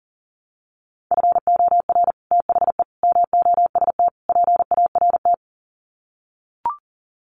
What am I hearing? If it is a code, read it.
Morse code, heard as a keyed tone, about 40 wpm, 700 Hz. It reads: FOR THE MOST PART
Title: Sets of 4 Words 40wpm